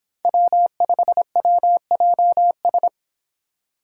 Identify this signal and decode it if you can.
Morse code, heard as a keyed tone, about 26 wpm, 700 Hz. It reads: W5WJS